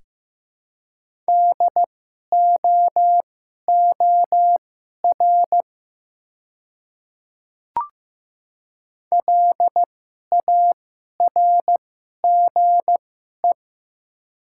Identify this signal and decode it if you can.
Morse code, heard as a keyed tone, about 15 wpm, spaced as for 11 wpm, 700 Hz. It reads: DOOR LARGE
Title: 2x Character Spacing - Top 300 Words - 15wpm